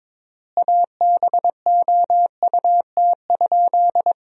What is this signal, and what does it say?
Morse code, heard as a keyed tone, about 22 wpm, 700 Hz. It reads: ABOUT?
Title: 3x Word-Spacing - Sentences from Top 100 Words - 22wpm